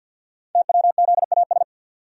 Morse code, heard as a keyed tone, about 51 wpm, 700 Hz. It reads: TWZAS
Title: Code Groups - 5 Letters - 50wpm